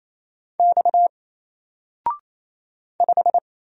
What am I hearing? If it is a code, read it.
Morse code, heard as a keyed tone, about 27 wpm, 700 Hz. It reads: X 5
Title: Single Letter-Number Alt 28wpm